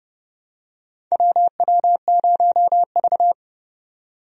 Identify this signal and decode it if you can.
Morse code, heard as a keyed tone, about 30 wpm, 700 Hz. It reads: WW0V
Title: USA Call Signs - 4 Characters - 30wpm